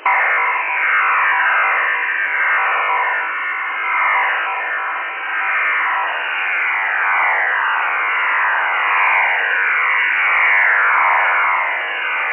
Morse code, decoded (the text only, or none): none